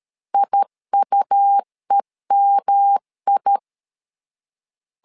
Morse code, heard as a keyed tone, about 13 wpm, 800 Hz. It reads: IUEMI
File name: LEZIONI / 3 MWRO / 3 MWR 12wpm 1711